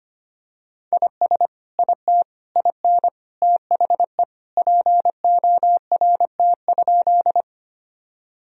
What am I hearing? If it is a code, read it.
Morse code, heard as a keyed tone, about 25 wpm, 700 Hz. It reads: IS IT IN THE PORT?
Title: Sentences from Top 1000 Words - Review Part 1 - 25wpm